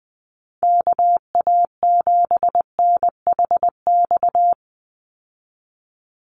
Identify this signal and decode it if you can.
Morse code, heard as a keyed tone, about 20 wpm, 700 Hz. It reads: KA7NHX
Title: USA Call Signs 20wpm